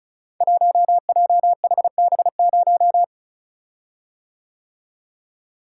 Morse code, heard as a keyed tone, about 34 wpm, 700 Hz. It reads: 1JHB0
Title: Code Groups - 5 Letter:Numbers - 35wpm